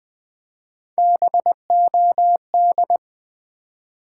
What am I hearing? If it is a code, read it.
Morse code, heard as a keyed tone, about 20 wpm, 700 Hz. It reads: BOD
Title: Build A Word - 1 Letter At A Time - 4-Letter Words - 20wpm